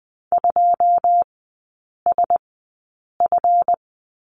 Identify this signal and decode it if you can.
Morse code, heard as a keyed tone, about 20 wpm, spaced as for 8 wpm, 700 Hz. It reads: 2SF
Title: 145 Lesson 24e - Letter y - Three Characters 20wpm